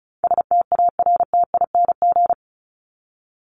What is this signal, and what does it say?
Morse code, heard as a keyed tone, about 34 wpm, 700 Hz. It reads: STARTING
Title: ING Suffix - Top 100 Words with Base Words 35wpm